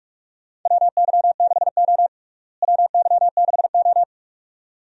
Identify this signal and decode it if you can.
Morse code, heard as a keyed tone, about 44 wpm, 700 Hz. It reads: WY6O WY6O
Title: USA Call Signs - 4 Characters - Repeated 2x - 45wpm